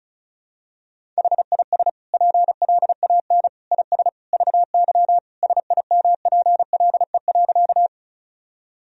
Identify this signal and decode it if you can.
Morse code, heard as a keyed tone, about 36 wpm, 700 Hz. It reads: HIS PLAN IS VY SIMPLE.
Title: Sentences from Top 500 Words and Abbreviations 35wpm